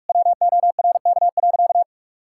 Morse code, heard as a keyed tone, about 46 wpm, 700 Hz. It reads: WORK.
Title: Sentences from Top 1000 Words - Review Part 5 - 45wpm